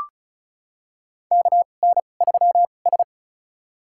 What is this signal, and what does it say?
Morse code, heard as a keyed tone, about 36 wpm, spaced as for 25 wpm, 700 Hz. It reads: KN3S
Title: USA Call Signs - 2x Character Spacing - 35wpm